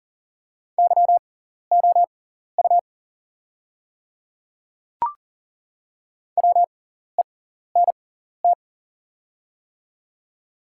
Morse code, heard as a keyed tone, about 40 wpm, spaced as for 14 wpm, 700 Hz. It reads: YOU WENT
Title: 6x Character Spacing - Top 200 Words - 40wpm